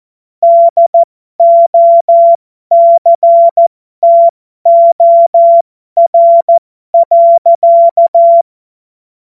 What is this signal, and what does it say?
Morse code, heard as a keyed tone, about 14 wpm, spaced as for 12 wpm, 680 Hz. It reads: DOCTOR.
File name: Morsekurs / Witze